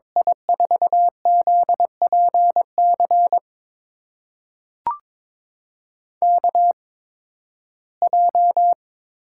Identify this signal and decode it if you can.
Morse code, heard as a keyed tone, about 22 wpm, spaced as for 14 wpm, 700 Hz. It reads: I4ZPC KJ
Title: USA Call Signs - 6 Characters - 8x Character Spacing - 22wpm